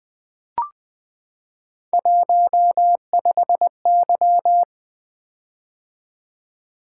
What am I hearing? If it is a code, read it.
Morse code, heard as a keyed tone, about 20 wpm, 700 Hz. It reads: 15Y
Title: Three Letter-Number Combinations - 20wpm